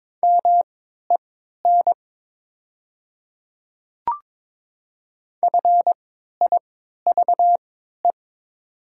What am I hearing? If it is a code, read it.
Morse code, heard as a keyed tone, about 23 wpm, spaced as for 13 wpm, 690 Hz. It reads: MEN FIVE
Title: 3x Character Spacing - Top 500 Words - 22wpm